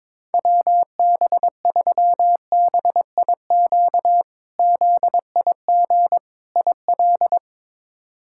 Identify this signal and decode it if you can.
Morse code, heard as a keyed tone, about 22 wpm, 700 Hz. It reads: WB3BIQ ZIG IL
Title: K1USN SST QSO Practice - 22wpm